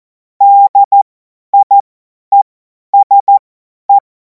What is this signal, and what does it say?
Morse code, heard as a keyed tone, about 14 wpm, spaced as for 10 wpm, 800 Hz. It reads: DIESE
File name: Morsekurs / Witze / Witz 28. April 2026 50 F70